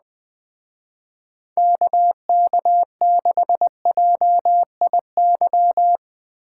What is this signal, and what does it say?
Morse code, heard as a keyed tone, about 20 wpm, 700 Hz. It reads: KK6JIY